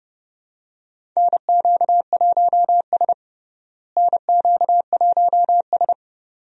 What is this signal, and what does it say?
Morse code, heard as a keyed tone, about 30 wpm, 700 Hz. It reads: NQ1S NQ1S
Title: USA Call Signs - 4 Characters - Repeated 2x - 30wpm